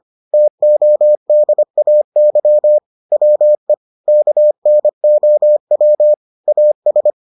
Morse code, heard as a keyed tone, about 25 wpm, 600 Hz. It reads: TODAY WE KNOW AS